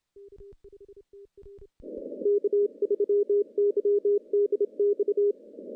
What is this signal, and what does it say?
Morse code, heard as a keyed tone, about 23 wpm, 410 Hz. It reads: K3YDX
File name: CW-SS-k5tr